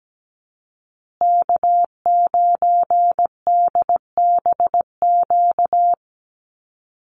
Morse code, heard as a keyed tone, about 17 wpm, 700 Hz. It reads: K9DBQ
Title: USA Call Signs Repeated 3x 17wpm